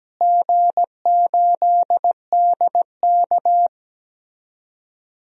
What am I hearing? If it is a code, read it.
Morse code, heard as a keyed tone, about 17 wpm, 700 Hz. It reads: G8DK
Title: DXCC-100 Call Signs - 4 Characters - Repeated 3x - 17wpm